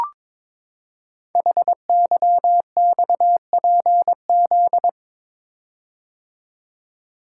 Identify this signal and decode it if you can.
Morse code, heard as a keyed tone, about 22 wpm, 700 Hz. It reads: HYXPZ
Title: Code Groups - 5 Letters - 22wpm